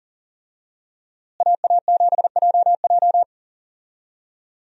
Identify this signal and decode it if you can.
Morse code, heard as a keyed tone, about 40 wpm, 700 Hz. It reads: AA7JJ
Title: USA Call Signs - 5 Characters - Repeated 3x - 40wpm